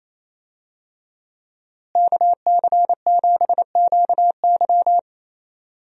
Morse code, heard as a keyed tone, about 28 wpm, 700 Hz. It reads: KC7QY
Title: USA Call Signs - 5 Characters - Repeated 2x - 28wpm